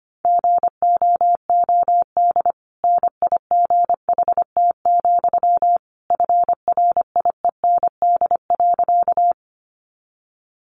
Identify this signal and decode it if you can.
Morse code, heard as a keyed tone, about 25 wpm, 700 Hz. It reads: GOOD NIGHT, FRIEND.
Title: Sentences from Top 300 Words 25wpm